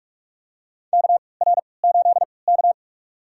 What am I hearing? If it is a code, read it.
Morse code, heard as a keyed tone, about 42 wpm, spaced as for 25 wpm, 700 Hz. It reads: KR8K
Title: USA Call Signs - 4 Characters - 3x Character Spacing - 45wpm